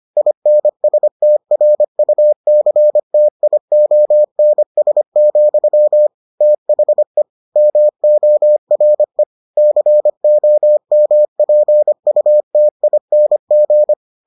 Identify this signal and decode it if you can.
Morse code, heard as a keyed tone, about 25 wpm, 600 Hz. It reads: INSTRUCTIONS, THE MORE COMPUTING